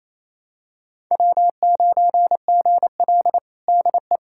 Morse code, heard as a keyed tone, about 28 wpm, 700 Hz. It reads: W9GL DE
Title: Practice QSO - Exchange 4 - Ending - 28wpm